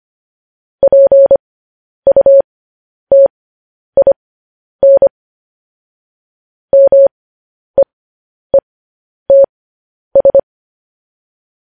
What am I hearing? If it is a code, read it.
Morse code, heard as a keyed tone, about 25 wpm, spaced as for 10 wpm, 550 Hz. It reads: PUTIN MEETS
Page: News Headlines in Morse Code at 10 WPM